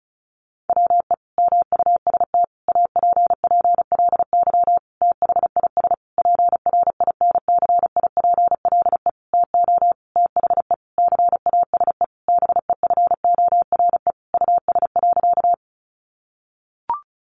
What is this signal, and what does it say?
Morse code, heard as a keyed tone, about 34 wpm, 700 Hz. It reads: WE MUST APPLY THIS PRINCIPLE TO THE CASE BEFORE US.
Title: Sentences from Top 1400 Words 35wpm